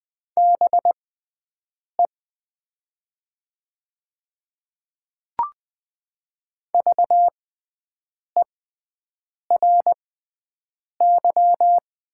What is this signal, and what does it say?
Morse code, heard as a keyed tone, about 20 wpm, spaced as for 7 wpm, 700 Hz. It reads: BE VERY
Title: 6x Character Spacing - Top 200 Words - 20wpm